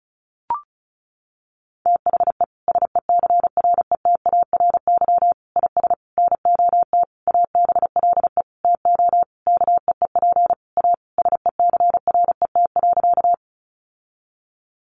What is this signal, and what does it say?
Morse code, heard as a keyed tone, about 34 wpm, 700 Hz. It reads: THE SECRETARY IS NOT ABLE TO KEEP A SECRET.
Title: Sentences from Top 1300 Words 35wpm